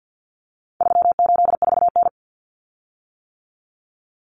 Morse code, heard as a keyed tone, about 48 wpm, 700 Hz. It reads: 384N
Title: Common Cut Numbers - 4-Digit 50wpm